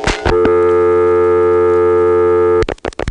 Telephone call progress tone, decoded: dial tone